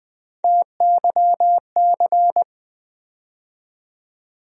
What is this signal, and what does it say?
Morse code, heard as a keyed tone, about 20 wpm, 700 Hz. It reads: TYC
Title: USA Call Sign Postfixes - 3 Letters 20wpm